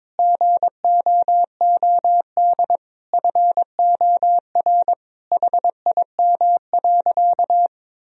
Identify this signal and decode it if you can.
Morse code, heard as a keyed tone, about 22 wpm, 700 Hz. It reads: GOOD FOR HIM.
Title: Sentences from Top 1000 Words - Review Part 1 - 22wpm